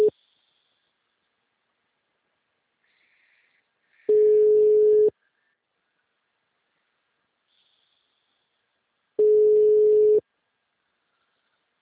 Ringback tone (Europe, 425 Hz).
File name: out- -2007- - - .11607